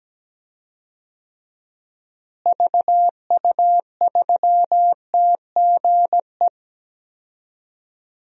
Morse code, heard as a keyed tone, about 17 wpm, 700 Hz. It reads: VU3TGE
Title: DXCC-100 Call Signs - 6 Characters - Repeated 2x - 17wpm